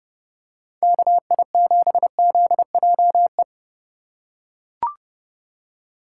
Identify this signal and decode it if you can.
Morse code, heard as a keyed tone, about 30 wpm, 700 Hz. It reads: KI7ZJE